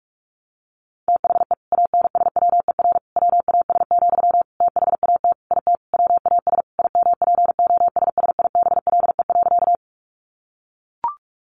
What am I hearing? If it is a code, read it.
Morse code, heard as a keyed tone, about 44 wpm, 700 Hz. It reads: THE ANSWER WAS, THAT IT WAS IMPOSSIBLE.
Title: Sentences from Top 1100 Words 45wpm